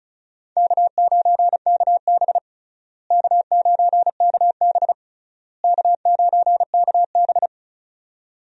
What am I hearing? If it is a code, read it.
Morse code, heard as a keyed tone, about 34 wpm, 700 Hz. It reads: K9KB K9KB K9KB
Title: USA Call Signs - 4 Characters - Repeated 3x - 35wpm